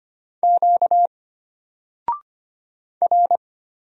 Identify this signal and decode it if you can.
Morse code, heard as a keyed tone, about 25 wpm, 700 Hz. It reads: Q R
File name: Single-Letter-Number-25wpm